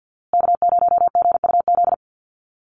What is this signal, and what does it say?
Morse code, heard as a keyed tone, about 50 wpm, 700 Hz. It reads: K0GUZ
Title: Call Signs Repeated 3x - 50wpm